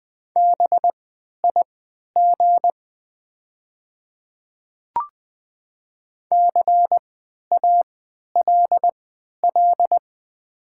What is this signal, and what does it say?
Morse code, heard as a keyed tone, about 21 wpm, spaced as for 11 wpm, 690 Hz. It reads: BIG CALL
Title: 3x Character Spacing - Top 300 Words - 20wpm